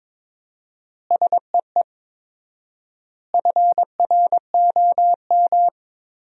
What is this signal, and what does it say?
Morse code, heard as a keyed tone, about 22 wpm, 700 Hz. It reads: SEE FROM
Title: 4x Word-Spacing - Sentences from Top 100 Words - 22wpm